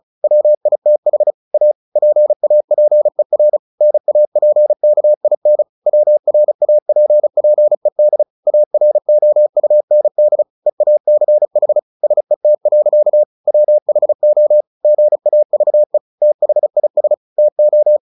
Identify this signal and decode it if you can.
Morse code, heard as a keyed tone, about 34 wpm, 600 Hz. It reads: WITH A PAPER NAPKIN WRAPPED AROUND EACH SET. WHO GAVE THIS TO
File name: the old man and the sea / the old man and the sea 1 35wpm